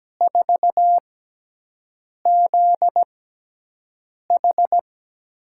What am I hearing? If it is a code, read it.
Morse code, heard as a keyed tone, about 17 wpm, spaced as for 6 wpm, 700 Hz. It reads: 4ZH
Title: USA Call Signs - 5 Characters - 6x Character Spacing - 17wpm